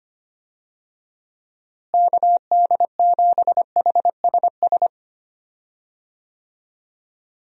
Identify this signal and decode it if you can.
Morse code, heard as a keyed tone, about 25 wpm, 700 Hz. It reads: KD7HSS
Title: USA Call Signs - 6 Characters - Repeated 2x - 25wpm